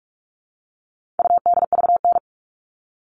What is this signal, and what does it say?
Morse code, heard as a keyed tone, about 44 wpm, 700 Hz. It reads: UDVN
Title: All Cut Numbers - 4-Digit 45wpm